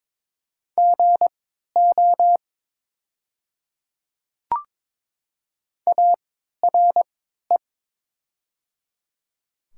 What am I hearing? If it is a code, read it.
Morse code, heard as a keyed tone, about 22 wpm, spaced as for 12 wpm, 700 Hz. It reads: GO ARE
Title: 3x Character Spacing - Top 300 Words - 22wpm